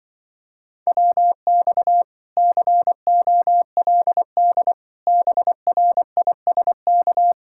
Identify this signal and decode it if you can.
Morse code, heard as a keyed tone, about 24 wpm, 700 Hz. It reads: WX COLD BRISK
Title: Speed-Racing Common QSO Phrases - 20wpm